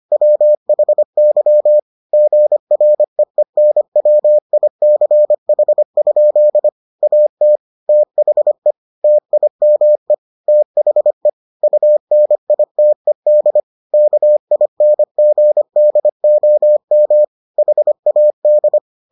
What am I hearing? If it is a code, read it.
Morse code, heard as a keyed tone, about 25 wpm, 600 Hz. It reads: WHY GREENWICH? AT THE TIME THE UNITED KINGDOM HAD